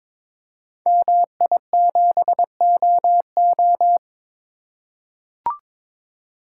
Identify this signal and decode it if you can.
Morse code, heard as a keyed tone, about 23 wpm, 700 Hz. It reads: MI7OO
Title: Code Groups - 5 Letter:Numbers - 22wpm